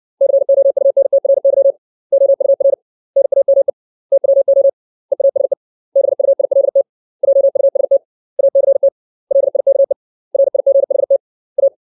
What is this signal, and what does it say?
Morse code, heard as a keyed tone, about 60 wpm, 550 Hz. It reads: COUNTRY OUR NAME TWO EASE BUILT JUST ACT LIKE RIGHT A
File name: 632 words / 632 Words 60WPM